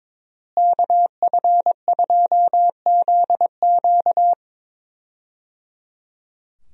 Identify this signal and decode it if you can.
Morse code, heard as a keyed tone, about 22 wpm, 700 Hz. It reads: KF2ZQ
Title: USA Call Signs - 5 Characters - Repeated 3x - 22wpm